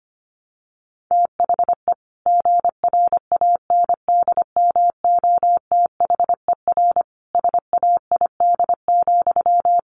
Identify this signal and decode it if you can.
Morse code, heard as a keyed tone, about 25 wpm, 700 Hz. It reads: THE GRANDMOTHER SAID,
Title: Sentences from Top 1200 Words 25wpm